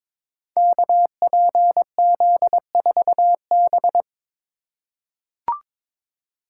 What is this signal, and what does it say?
Morse code, heard as a keyed tone, about 22 wpm, 700 Hz. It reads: KPZ4B
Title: Code Groups - 5 Letter:Numbers - 22wpm